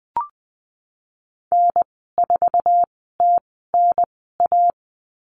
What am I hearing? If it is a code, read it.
Morse code, heard as a keyed tone, about 19 wpm, spaced as for 14 wpm, 700 Hz. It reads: N4TNA
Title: 048 Lesson 9g - Number 4 - Call Signs 20wpm